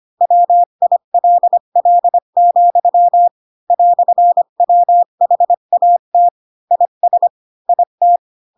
Morse code, heard as a keyed tone, about 25 wpm, 700 Hz. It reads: WILL, "WHAT IS IT